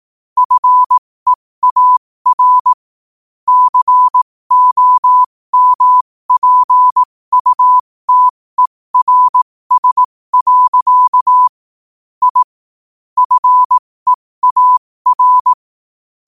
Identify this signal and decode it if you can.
Morse code, heard as a keyed tone, about 18 wpm, spaced as for 15 wpm, 1000 Hz. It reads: FEAR COMPUTERS. I FEAR
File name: qotdfiles / 15wpm